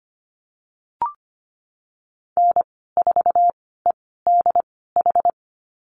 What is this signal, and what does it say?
Morse code, heard as a keyed tone, about 25 wpm, spaced as for 16 wpm, 700 Hz. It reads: N4EDH
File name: Lesson 15g - Number 5 - Call Signs 25wpm